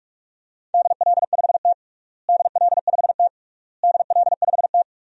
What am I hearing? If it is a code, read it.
Morse code, heard as a keyed tone, about 46 wpm, 700 Hz. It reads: DL5T DL5T DL5T
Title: DXCC-100 Call Signs - 4 Characters - Repeated 3x - 45wpm